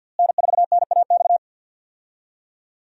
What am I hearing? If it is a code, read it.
Morse code, heard as a keyed tone, about 50 wpm, 700 Hz. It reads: N4NAX
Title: USA Call Signs - 5 Characters - Repeated 3x - 50wpm